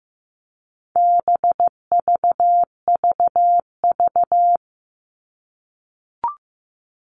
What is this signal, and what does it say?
Morse code, heard as a keyed tone, about 15 wpm, 700 Hz. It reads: BVVV
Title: All Cut Numbers - 4-Digit 15wpm